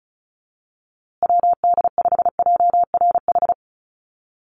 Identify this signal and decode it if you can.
Morse code, heard as a keyed tone, about 34 wpm, 700 Hz. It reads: WD5JRH
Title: USA Call Signs Repeated 2x 35wpm